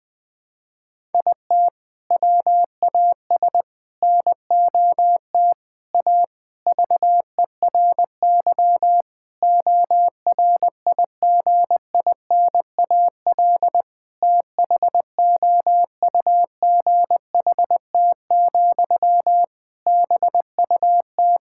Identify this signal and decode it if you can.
Morse code, heard as a keyed tone, about 21 wpm, 700 Hz. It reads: IT WAS NOT A VERY ORIGINAL THOUGHT, BUT
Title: Sentences from Top 1000 Words - Review Part 6 - 20wpm